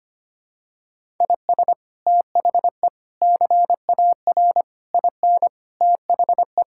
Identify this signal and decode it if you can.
Morse code, heard as a keyed tone, about 25 wpm, 700 Hz. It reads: IS THE CAR IN THE
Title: Sentences from Top 1600 Words 25wpm